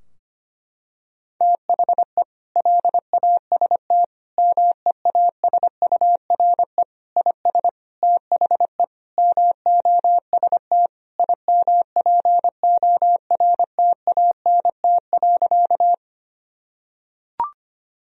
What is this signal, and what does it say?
Morse code, heard as a keyed tone, about 25 wpm, 700 Hz. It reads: THE LAST MEASURE IS THE MOST IMPORTANT.
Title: Sentences from Top 1000 Words - Review Part 2 - 25wpm